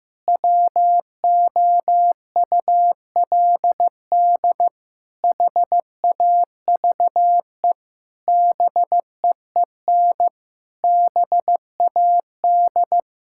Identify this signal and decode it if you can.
Morse code, heard as a keyed tone, about 15 wpm, 700 Hz. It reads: WOULD HAVE BEEN BAD
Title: Sentences from Top 1000 Words - Review Part 5 - 15wpm